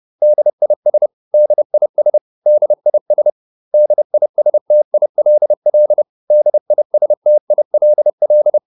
Morse code, heard as a keyed tone, about 30 wpm, 600 Hz. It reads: DIS DIS DIS DISTILL DISTILL